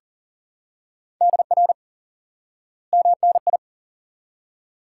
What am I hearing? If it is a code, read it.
Morse code, heard as a keyed tone, about 40 wpm, 700 Hz. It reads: DR MNI
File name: IQE Course 3 - Lesson 89 - YAGI - Two Elements - 40wpm